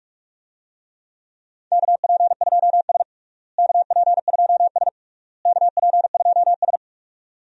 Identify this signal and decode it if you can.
Morse code, heard as a keyed tone, about 41 wpm, 700 Hz. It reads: KP2S KP2S KP2S
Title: DXCC-100 Call Signs - 4 Characters - Repeated 3x - 45wpm